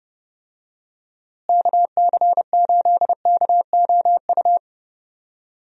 Morse code, heard as a keyed tone, about 30 wpm, 700 Hz. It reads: KC8KOU
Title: USA Call Signs - 6 Characters - Repeated 3x - 30wpm